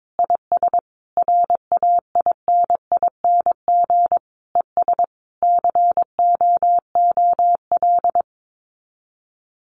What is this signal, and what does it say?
Morse code, heard as a keyed tone, about 22 wpm, 700 Hz. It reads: IS RAINING ES COOL
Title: Common QSO Phrases 22wpm